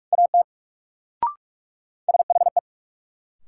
Morse code, heard as a keyed tone, about 42 wpm, 700 Hz. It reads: AT SHE
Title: Sentences from Top 100 Words - New Word Drill - 45wpm